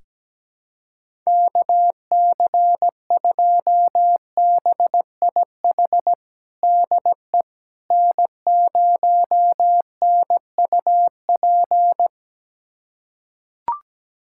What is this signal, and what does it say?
Morse code, heard as a keyed tone, about 17 wpm, 700 Hz. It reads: KC2BIH DE N0NUP
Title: USA Callsign DE Callsign 17wpm